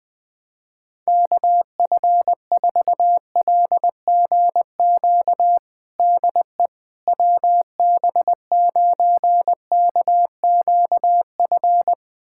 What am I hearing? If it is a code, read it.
Morse code, heard as a keyed tone, about 21 wpm, 700 Hz. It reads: KF4LGQ DE WB9KQF
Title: USA Callsign DE Callsign - 6 Characters - 20wpm